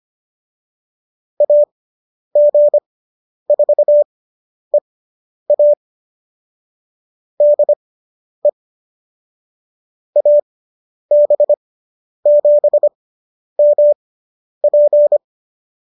Morse code, heard as a keyed tone, about 25 wpm, spaced as for 10 wpm, 570 Hz. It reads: AG4EA DE AB7MP